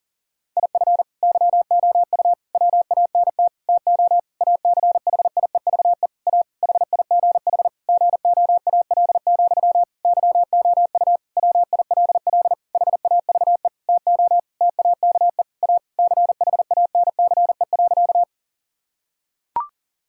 Morse code, heard as a keyed tone, about 37 wpm, 690 Hz. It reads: IF YOU WANT TO ACHIEVE A HIGH GOAL, YOU WILL HAVE TO TAKE A CHANCE.
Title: Sentences from Top 1000 Words - Review Part 6 - 40wpm